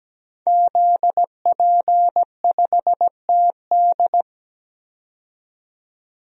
Morse code, heard as a keyed tone, about 17 wpm, 700 Hz. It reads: ZP5TD
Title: DXCC-100 Call Signs - 5 Characters - 17wpm